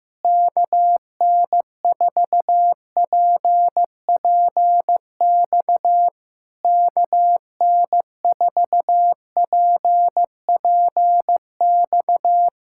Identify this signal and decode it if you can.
Morse code, heard as a keyed tone, about 15 wpm, 700 Hz. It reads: KN4PPX KN4PPX
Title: Speed-Racing Common QSO Phrases Encore - 15wpm